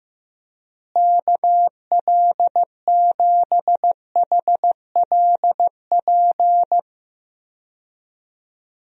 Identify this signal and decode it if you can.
Morse code, heard as a keyed tone, about 15 wpm, 700 Hz. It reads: KL7HLP
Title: DXCC-100 Call Signs - 6 Characters - Repeated 3x - 15wpm